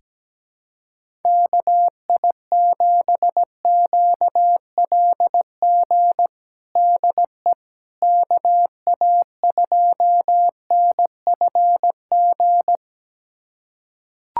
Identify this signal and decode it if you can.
Morse code, heard as a keyed tone, about 17 wpm, 700 Hz. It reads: KI7QLG DE KA2NFG
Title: USA Callsign DE Callsign - 6 Characters - 17wpm